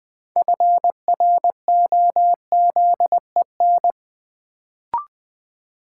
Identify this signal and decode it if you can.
Morse code, heard as a keyed tone, about 20 wpm, 700 Hz. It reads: FROZEN
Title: Build A Word - 2 Letters At A Time - 6-Letter Words - 20wpm